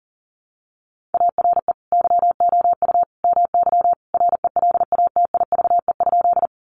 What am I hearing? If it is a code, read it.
Morse code, heard as a keyed tone, about 40 wpm, 700 Hz. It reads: ARE YOU MY RELATIVE?
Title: Sentences from Top 1400 Words 40wpm